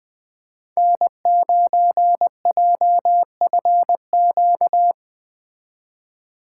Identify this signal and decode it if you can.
Morse code, heard as a keyed tone, about 21 wpm, 700 Hz. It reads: N9JFQ